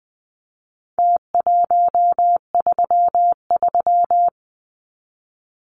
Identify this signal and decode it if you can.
Morse code, heard as a keyed tone, about 20 wpm, 700 Hz. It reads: T133
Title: Numbers - Contest Style - 4-Digit 20wpm